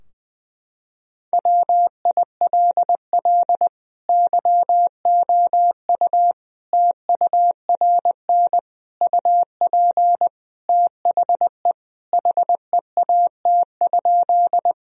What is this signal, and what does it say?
Morse code, heard as a keyed tone, about 20 wpm, 700 Hz. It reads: WILL YOU TURN UP THE HEAT?
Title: Sentences from Top 1000 Words - Review Part 3 - 20wpm